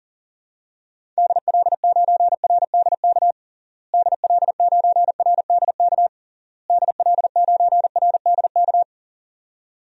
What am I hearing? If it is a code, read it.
Morse code, heard as a keyed tone, about 40 wpm, 700 Hz. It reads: DL9RDK DL9RDK DL9RDK
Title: DXCC-100 Call Signs - 6 Characters - Repeated 3x - 40wpm